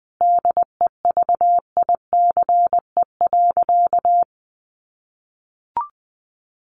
Morse code, heard as a keyed tone, about 20 wpm, 700 Hz. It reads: DEVICE.
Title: Sentences from Top 1500 Words 20wpm